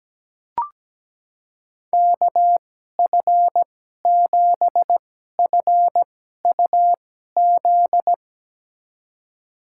Morse code, heard as a keyed tone, about 17 wpm, spaced as for 12 wpm, 700 Hz. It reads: KF7FUZ
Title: USA Call Signs - 6 Characters - 2x Character Spacing - 17wpm